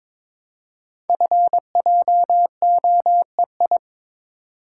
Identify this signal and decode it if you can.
Morse code, heard as a keyed tone, about 22 wpm, 700 Hz. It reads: FJOEI
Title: Code Groups - 5 Letters - 22wpm